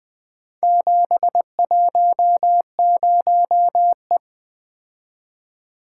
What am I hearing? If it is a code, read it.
Morse code, heard as a keyed tone, about 20 wpm, 700 Hz. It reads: 710E
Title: Four Letter-Number Combinations - 20wpm